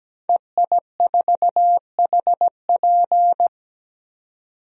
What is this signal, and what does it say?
Morse code, heard as a keyed tone, about 17 wpm, 700 Hz. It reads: EI4HP
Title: DXCC-100 Call Signs - 5 Characters - 17wpm